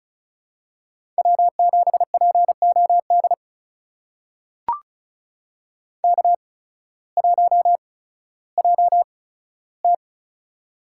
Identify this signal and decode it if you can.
Morse code, heard as a keyed tone, about 34 wpm, spaced as for 16 wpm, 700 Hz. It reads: W7POD K1JT
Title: USA Call Signs - 8x Character Spacing - 35wpm